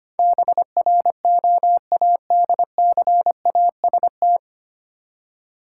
Morse code, heard as a keyed tone, about 25 wpm, 700 Hz. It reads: BROADCAST
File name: Nine Letter Words - 25wpm